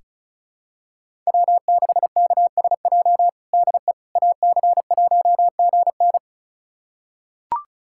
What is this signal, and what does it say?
Morse code, heard as a keyed tone, about 36 wpm, 700 Hz. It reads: W6KSJ DE AC1GN
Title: USA Callsign DE Callsign - 5 Characters - 35wpm